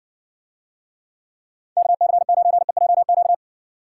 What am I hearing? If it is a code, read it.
Morse code, heard as a keyed tone, about 60 wpm, 700 Hz. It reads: KC9EJQ